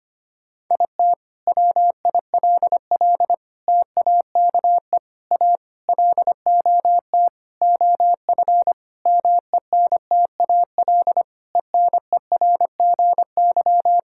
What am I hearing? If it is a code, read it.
Morse code, heard as a keyed tone, about 25 wpm, 700 Hz. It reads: IT WILL TAKE A LOT OF MENTAL ENERGY
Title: Sentences from Top 1000 Words - Review Part 6 - 25wpm